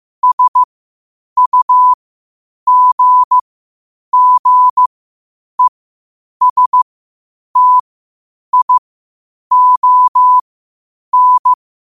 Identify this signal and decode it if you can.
Morse code, heard as a keyed tone, about 15 wpm, spaced as for 8 wpm, 1000 Hz. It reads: SUGGESTION
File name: qotdfiles / 8wpm